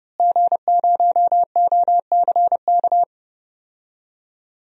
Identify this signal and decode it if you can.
Morse code, heard as a keyed tone, about 31 wpm, 700 Hz. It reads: G0OCK
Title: DXCC-100 Call Signs - 5 Characters - Repeated 2x - 30wpm